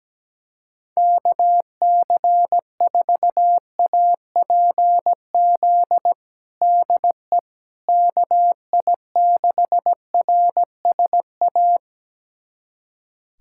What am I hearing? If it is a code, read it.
Morse code, heard as a keyed tone, about 17 wpm, 700 Hz. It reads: KC4APZ DE KI6RSA
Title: USA Callsign DE Callsign - 6 Characters - 17wpm